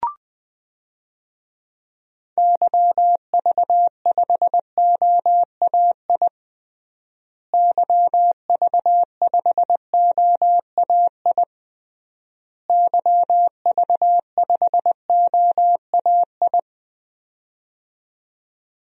Morse code, heard as a keyed tone, about 20 wpm, 700 Hz. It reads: YV5OAI YV5OAI YV5OAI
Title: DXCC-100 Call Signs - 6 Characters - Repeated 3x - 20wpm